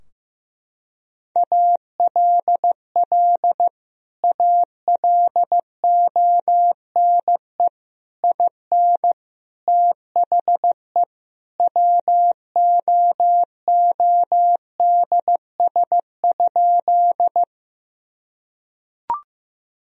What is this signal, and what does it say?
Morse code, heard as a keyed tone, about 16 wpm, 700 Hz. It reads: ALL ALONE IN THE WOODS?
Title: Sentences from Top 1000 Words - Review Part 2 - 15wpm